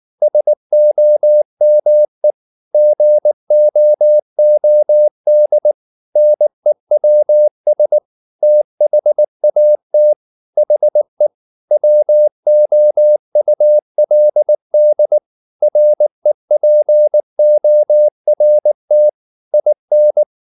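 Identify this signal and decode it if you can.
Morse code, heard as a keyed tone, about 19 wpm, 600 Hz. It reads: SOME GOOD NEWS THAT HE WOULD REPORT IN
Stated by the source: Morse code at 19 wpm from CW med Gnister, reading Ken Follett's novel Never in English.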